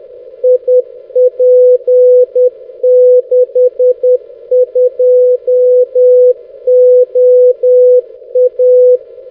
Morse code, heard as a keyed tone, about 10 wpm, 500 Hz. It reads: IP62OA
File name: 6m-oy6bec-tf1six-jw7six-1810z